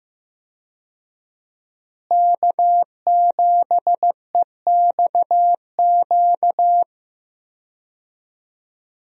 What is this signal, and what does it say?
Morse code, heard as a keyed tone, about 15 wpm, 700 Hz. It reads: K7EXQ
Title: USA Call Signs - 5 Characters - Repeated 3x - 15wpm